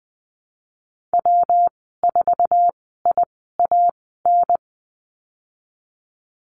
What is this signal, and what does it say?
Morse code, heard as a keyed tone, about 20 wpm, spaced as for 15 wpm, 700 Hz. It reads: W4IAN
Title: 147 Lesson 24g - Letter y - Call Signs 20wpm